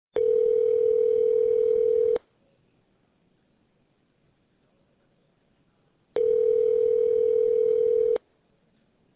Ringback tone (North America), 440 + 480 Hz.